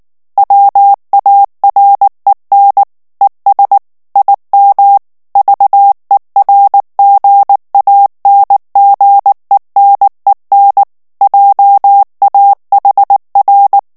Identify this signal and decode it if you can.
Morse code, heard as a keyed tone, about 19 wpm, 800 Hz. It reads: WAREN ES IM VERGANGENEN JAHR
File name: morse mp3 / ALLEMAND texte clair 2 vitesse 18 mots minute